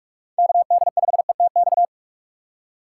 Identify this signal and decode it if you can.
Morse code, heard as a keyed tone, about 46 wpm, 700 Hz. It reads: KD5ETX
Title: USA Call Signs - 6 Characters - 6x Character Spacing - 45wpm